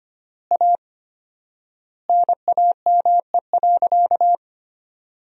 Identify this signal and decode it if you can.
Morse code, heard as a keyed tone, about 25 wpm, 700 Hz. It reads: A NAME.
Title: 4x Word-Spacing - Sentences from Top 500 Words - 25wpm